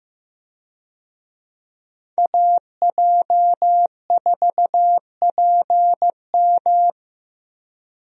Morse code, heard as a keyed tone, about 15 wpm, 700 Hz. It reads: AJ4PM
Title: USA Call Signs - 5 Characters - Repeated 3x - 15wpm